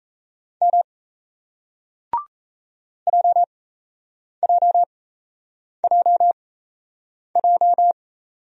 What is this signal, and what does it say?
Morse code, heard as a keyed tone, about 38 wpm, spaced as for 8 wpm, 700 Hz. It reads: M JJJJ